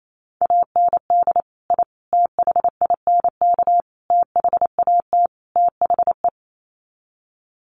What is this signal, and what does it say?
Morse code, heard as a keyed tone, about 28 wpm, 700 Hz. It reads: AND I THINK THAT THE
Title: Sets of 5 Words 28wpm